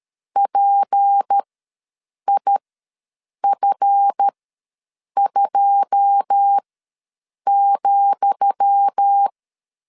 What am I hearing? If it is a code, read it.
Morse code, heard as a keyed tone, about 13 wpm, spaced as for 7 wpm, 800 Hz. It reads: PIF2,
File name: LEZIONI / 10 SIGNS / 10D COMA ws 12wpm 3618